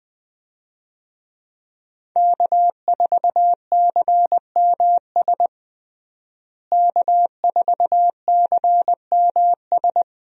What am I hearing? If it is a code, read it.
Morse code, heard as a keyed tone, about 20 wpm, 700 Hz. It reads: K4CMS K4CMS
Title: USA Call Signs - 5 Characters - Repeated 2x - 20wpm